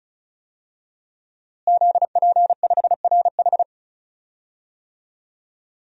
Morse code, heard as a keyed tone, about 36 wpm, 700 Hz. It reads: ZP5RH
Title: DXCC-100 Call Signs - 5 Characters - Repeated 3x - 35wpm